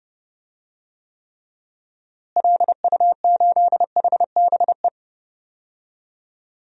Morse code, heard as a keyed tone, about 29 wpm, 700 Hz. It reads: LU8HBE
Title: DXCC-100 Call Signs - 6 Characters - Repeated 3x - 30wpm